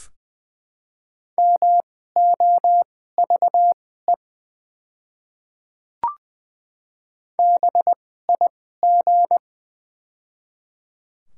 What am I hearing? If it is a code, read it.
Morse code, heard as a keyed tone, about 21 wpm, spaced as for 15 wpm, 700 Hz. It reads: MOVE BIG
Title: 2x Character Spacing - Top 200 Words - 20wpm